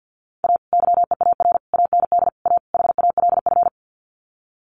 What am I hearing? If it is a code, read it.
Morse code, heard as a keyed tone, about 50 wpm, 700 Hz. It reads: A YEAR AND A HALF
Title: Sets of 5 Words 50wpm